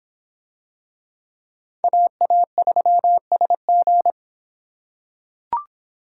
Morse code, heard as a keyed tone, about 25 wpm, 700 Hz. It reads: AA3SG